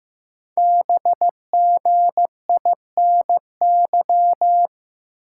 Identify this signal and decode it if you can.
Morse code, heard as a keyed tone, about 15 wpm, 700 Hz. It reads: BGINY
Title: Code Groups - 5 Letters - 15wpm